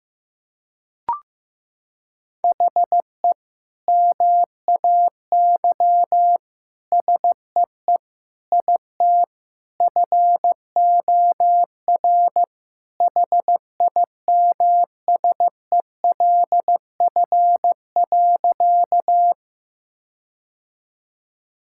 Morse code, heard as a keyed tone, about 16 wpm, 700 Hz. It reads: HE MAY SEE IT FOR HIMSELF.
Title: Sentences from Top 1600 Words 15wpm